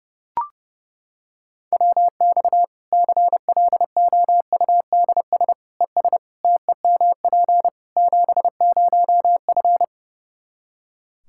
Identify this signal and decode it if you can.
Morse code, heard as a keyed tone, about 30 wpm, 700 Hz. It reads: WX CLOUDS ES TEMP 70F
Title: Practice QSO - Exchange 2 - Info - 30wpm